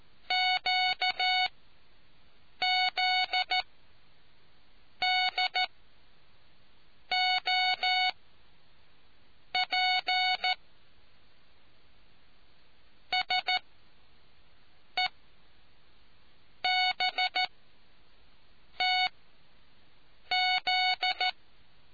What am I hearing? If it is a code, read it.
Morse code, heard as a keyed tone, about 14 wpm, spaced as for 5 wpm, 740 Hz. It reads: QZDOP SEBTZ